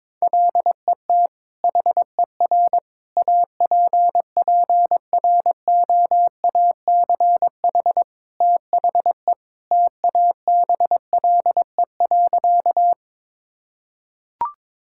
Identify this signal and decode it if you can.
Morse code, heard as a keyed tone, about 21 wpm, 700 Hz. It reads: LET HER APPROACH THE TABLE.
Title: Sentences from Top 1000 Words - Review Part 6 - 22wpm